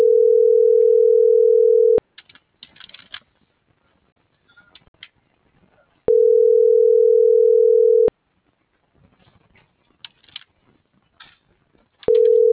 Ringback tone (North America), 440 + 480 Hz.